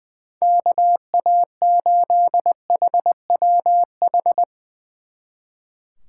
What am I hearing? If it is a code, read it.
Morse code, heard as a keyed tone, about 21 wpm, 700 Hz. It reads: KA8HWH
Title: Speed Racing - USA 2x3 Call Signs 20wpm